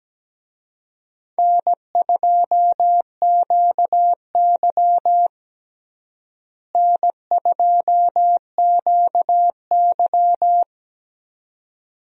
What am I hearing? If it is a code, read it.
Morse code, heard as a keyed tone, about 17 wpm, 700 Hz. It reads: N2QY N2QY
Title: USA Call Signs - 4 Characters - Repeated 2x - 17wpm